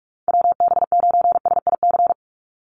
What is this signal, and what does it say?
Morse code, heard as a keyed tone, about 44 wpm, 700 Hz. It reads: WB9SIC
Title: USA Call Signs 45wpm